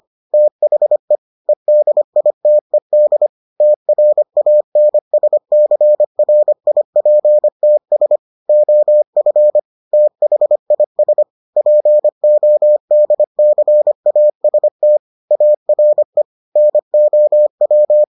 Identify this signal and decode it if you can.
Morse code, heard as a keyed tone, about 25 wpm, 600 Hz. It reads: THE EDITED TRANSCRIPTS OF THIS PODCAST ARE NOW